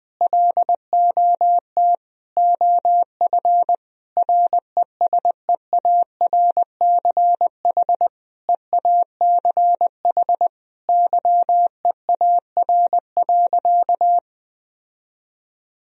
Morse code, heard as a keyed tone, about 19 wpm, 700 Hz. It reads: LOT OF RESEARCH EACH YEAR.
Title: Sentences from Top 1000 Words - Review Part 4 - 20wpm